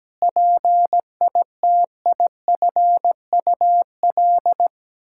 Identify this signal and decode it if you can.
Morse code, heard as a keyed tone, about 17 wpm, 700 Hz. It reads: PITIFUL
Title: Pangram Challenge v2 - 17wpm